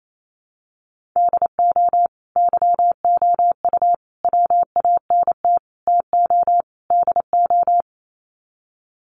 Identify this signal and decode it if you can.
Morse code, heard as a keyed tone, about 28 wpm, 700 Hz. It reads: DO YOU WANT TO DO
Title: Sets of 5 Words Encore 28wpm